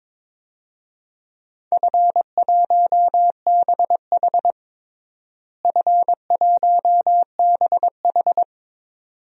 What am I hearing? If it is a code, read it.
Morse code, heard as a keyed tone, about 22 wpm, 700 Hz. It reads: F1BH F1BH
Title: DXCC-100 Call Signs - 4 Characters - Repeated 2x - 22wpm